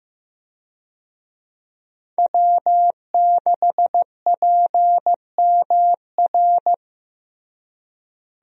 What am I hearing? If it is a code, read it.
Morse code, heard as a keyed tone, about 16 wpm, 700 Hz. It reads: W6PMR
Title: USA Call Signs - 5 Characters - Repeated 3x - 15wpm